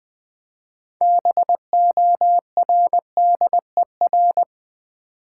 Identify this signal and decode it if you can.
Morse code, heard as a keyed tone, about 20 wpm, 700 Hz. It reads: BORDER
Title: 6 Letter Words - First 3 Letters Given - 20wpm